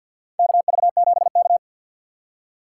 Morse code, heard as a keyed tone, about 52 wpm, 700 Hz. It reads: KV7K
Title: USA Call Signs - 4 Characters - 2x Character Spacing - 50wpm